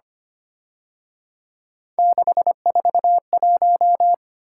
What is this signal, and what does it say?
Morse code, heard as a keyed tone, about 29 wpm, 700 Hz. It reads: 641
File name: Speed-Racing - Three Letter-Number Combinations - 25wpm